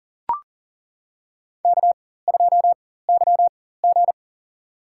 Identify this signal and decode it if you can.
Morse code, heard as a keyed tone, about 40 wpm, spaced as for 19 wpm, 700 Hz. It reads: K2YG
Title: USA Call Signs - 4 Characters - 4x Character Spacing - 40wpm